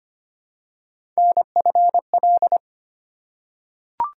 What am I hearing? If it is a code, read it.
Morse code, heard as a keyed tone, about 25 wpm, 700 Hz. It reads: NFL
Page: morse-code-ninja-practice-25wpm